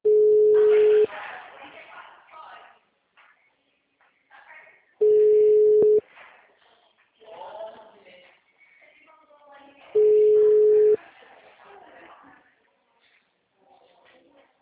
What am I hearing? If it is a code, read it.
Ringback tone (Europe, 425 Hz), with speech.
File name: out- -2007- - - .221